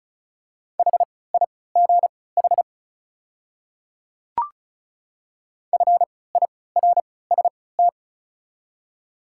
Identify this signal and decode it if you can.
Morse code, heard as a keyed tone, about 34 wpm, spaced as for 20 wpm, 700 Hz. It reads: HIGH FIRST
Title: 3x Character Spacing - Top 300 Words - 35wpm